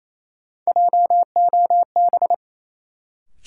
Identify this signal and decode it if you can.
Morse code, heard as a keyed tone, about 28 wpm, 700 Hz. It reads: JOB
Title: Sentences from Top 600 Words - New Word Drill - 28wpm